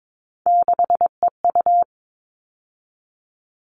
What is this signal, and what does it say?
Morse code, heard as a keyed tone, about 22 wpm, 700 Hz. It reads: 6EU
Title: All Cut Numbers - 3-Digit 22wpm